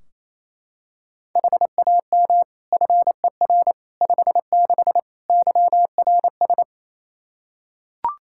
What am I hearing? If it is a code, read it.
Morse code, heard as a keyed tone, about 28 wpm, 700 Hz. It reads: HAM FER 56 YRS
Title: Practice QSO - Exchange 3 - More Info - 28wpm